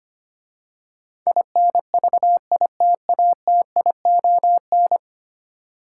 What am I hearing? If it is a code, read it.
Morse code, heard as a keyed tone, about 25 wpm, 700 Hz. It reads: INVITATION
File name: 10 Letter Words - First 3 Letters Given - 25wpm